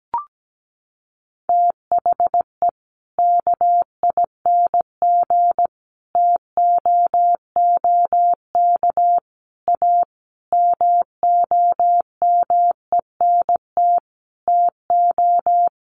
Morse code, heard as a keyed tone, about 17 wpm, 700 Hz. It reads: THE KING TOOK A MOMENT TO
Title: Sentences from Top 1500 Words 17wpm